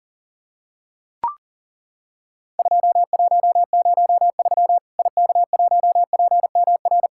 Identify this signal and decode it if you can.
Morse code, heard as a keyed tone, about 40 wpm, 700 Hz. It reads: 2103 IK1PMR
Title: Worked All Europe DX Contest - QTC Exchange - 40wpm